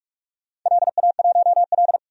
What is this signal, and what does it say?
Morse code, heard as a keyed tone, about 44 wpm, 700 Hz. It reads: LA1L
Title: DXCC-100 Call Signs - 4 Characters - 45wpm